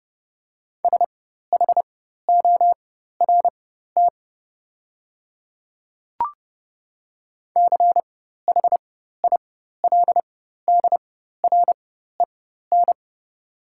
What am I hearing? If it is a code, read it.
Morse code, heard as a keyed tone, about 30 wpm, spaced as for 14 wpm, 700 Hz. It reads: SHORT CHILDREN
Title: 4x Character Spacing - Top 500 Words - 30wpm